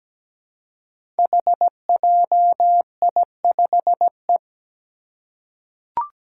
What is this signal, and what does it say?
Morse code, heard as a keyed tone, about 17 wpm, 700 Hz. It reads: HJI5E
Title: Five Challenging Letter-Number Combinations 17wpm